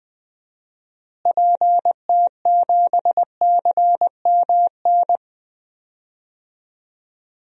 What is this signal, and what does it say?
Morse code, heard as a keyed tone, about 20 wpm, 700 Hz. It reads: PT7CMN
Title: DXCC-100 Call Signs - 6 Characters - Repeated 2x - 20wpm